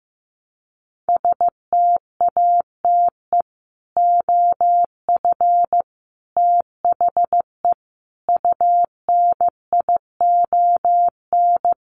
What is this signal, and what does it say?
Morse code, heard as a keyed tone, about 15 wpm, 700 Hz. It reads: STATE OF THE UNION
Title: Sets of 5 Words Encore 15wpm